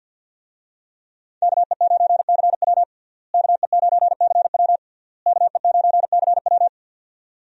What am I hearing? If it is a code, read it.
Morse code, heard as a keyed tone, about 49 wpm, 700 Hz. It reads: KE9CW KE9CW KE9CW
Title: USA Call Signs - 5 Characters - Repeated 3x - 50wpm